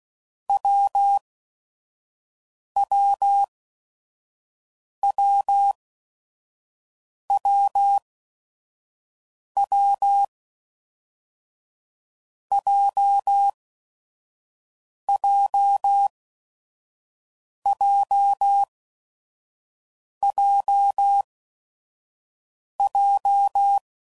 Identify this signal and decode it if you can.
Morse code, heard as a keyed tone, about 16 wpm, spaced as for 5 wpm, 800 Hz. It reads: WWWWW JJJJJ
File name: Combinations / Difficult Combos / WPJ WPJ1